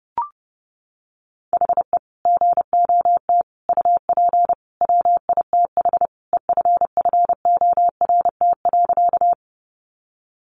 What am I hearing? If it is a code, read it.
Morse code, heard as a keyed tone, about 30 wpm, 700 Hz. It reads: HE GOT UP WITH EFFORT.
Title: Sentences from Top 1200 Words 30wpm